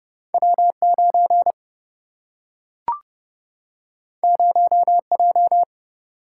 Morse code, heard as a keyed tone, about 30 wpm, 700 Hz. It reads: W 9 0 J
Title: Single-Letter Single-Number 30wpm